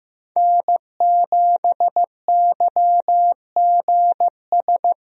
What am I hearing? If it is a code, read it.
Morse code, heard as a keyed tone, about 15 wpm, 700 Hz. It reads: N7YGS
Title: USA Call Signs - 5 Characters - Repeated 2x - 15wpm